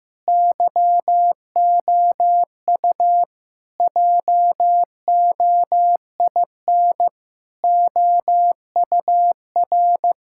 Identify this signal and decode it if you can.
Morse code, heard as a keyed tone, about 16 wpm, 700 Hz. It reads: YOU JOIN OUR
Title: Sentences from Top 1600 Words 15wpm